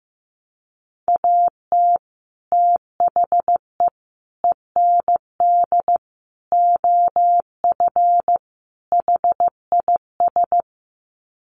Morse code, heard as a keyed tone, about 15 wpm, 700 Hz. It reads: AT THE END OF HIS
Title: Sets of 5 Words Encore 15wpm